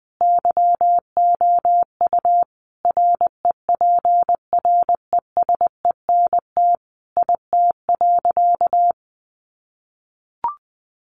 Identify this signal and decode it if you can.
Morse code, heard as a keyed tone, about 20 wpm, 700 Hz. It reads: YOU REPRESENT IT.
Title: Sentences from Top 1200 Words 20wpm